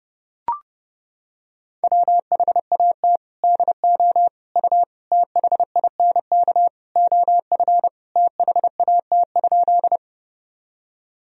Morse code, heard as a keyed tone, about 30 wpm, 700 Hz. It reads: WHAT DO U THINK OF THAT?
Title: Sentences from Top 200 Words and Abbreviations 30wpm